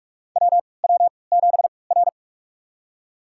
Morse code, heard as a keyed tone, about 46 wpm, spaced as for 26 wpm, 700 Hz. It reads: WW7R
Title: USA Call Signs - 4 Characters - 3x Character Spacing - 45wpm